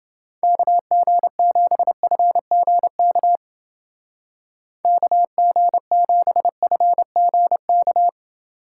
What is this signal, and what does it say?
Morse code, heard as a keyed tone, about 29 wpm, 700 Hz. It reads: KG7FGK KG7FGK